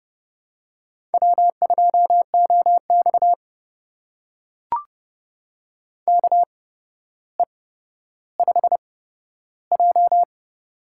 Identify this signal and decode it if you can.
Morse code, heard as a keyed tone, about 30 wpm, spaced as for 13 wpm, 700 Hz. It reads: W2OX KE5J
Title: USA Call Signs - 4 Characters - 8x Character Spacing - 30wpm